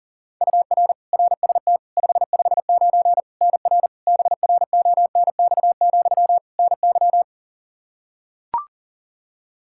Morse code, heard as a keyed tone, about 38 wpm, 700 Hz. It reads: UR RST 559 NR BRONX, NY
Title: Practice QSO - Exchange 1 - Beginning - 40wpm